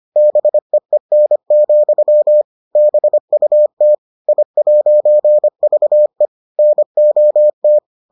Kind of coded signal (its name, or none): Morse code